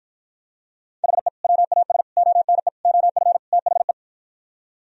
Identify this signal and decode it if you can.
Morse code, heard as a keyed tone, about 52 wpm, 700 Hz. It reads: HE WAS ONE OF THE